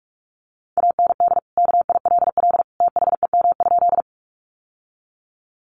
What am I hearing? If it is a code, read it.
Morse code, heard as a keyed tone, about 44 wpm, 700 Hz. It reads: AND KILL THEM?
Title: Sentences from Top 1400 Words 45wpm